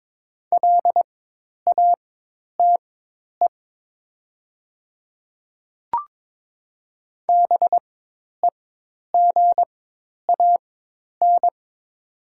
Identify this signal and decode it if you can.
Morse code, heard as a keyed tone, about 22 wpm, spaced as for 10 wpm, 700 Hz. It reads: LATE BEGAN
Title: 4x Character Spacing - Top 300 Words - 22wpm